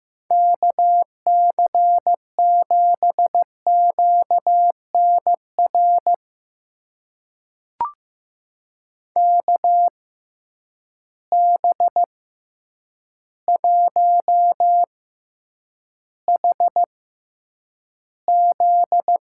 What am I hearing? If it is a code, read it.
Morse code, heard as a keyed tone, about 15 wpm, spaced as for 8 wpm, 700 Hz. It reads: KC7QNR KB1HZ
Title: USA Call Signs - 6 Characters - 6x Character Spacing - 15wpm